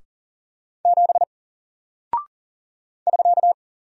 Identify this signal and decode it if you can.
Morse code, heard as a keyed tone, about 39 wpm, 700 Hz. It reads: 7 <SK>
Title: Single Character (Letter:Number:Punctuation:Prosign) 40wpm